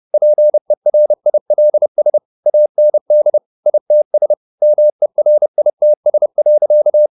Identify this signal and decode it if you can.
Morse code, heard as a keyed tone, about 30 wpm, 600 Hz. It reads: PERILS AND ITS MERITS.